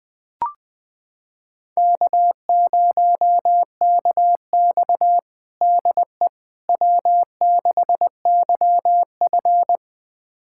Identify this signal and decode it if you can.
Morse code, heard as a keyed tone, about 20 wpm, 700 Hz. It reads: K0KX DE W6YF
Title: USA Callsign DE Callsign - 4 Characters - 20wpm